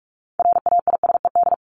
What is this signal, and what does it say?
Morse code, heard as a keyed tone, about 44 wpm, 700 Hz. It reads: RAISED
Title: ED Suffix - Top 100 Words with Base Words 45wpm